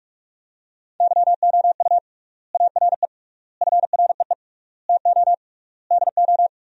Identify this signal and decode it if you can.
Morse code, heard as a keyed tone, about 46 wpm, 700 Hz. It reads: YOU ARE FREE TO DO
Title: 3x Word-Spacing - Sentences from Top 500 Words - 45wpm